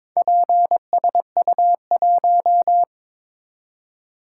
Morse code, heard as a keyed tone, about 22 wpm, 700 Hz. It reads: PSU1
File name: practice / 22wpm / Four Letter-Number Combinations - 22wpm